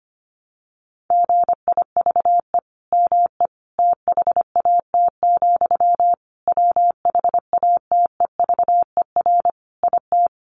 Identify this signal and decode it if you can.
Morse code, heard as a keyed tone, about 25 wpm, 700 Hz. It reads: GIVE ME THAT, WHATEVER IT
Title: Sentences from Top 1100 Words 25wpm